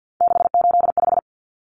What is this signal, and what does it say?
Morse code, heard as a keyed tone, about 50 wpm, 700 Hz. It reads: 685
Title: Numbers - 3-Digit 50wpm